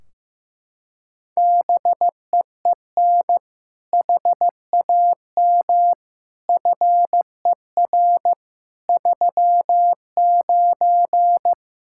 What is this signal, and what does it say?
Morse code, heard as a keyed tone, about 15 wpm, 700 Hz. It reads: BEEN HAM FER 39
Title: Practice QSO - Exchange 3 - More Info - 15wpm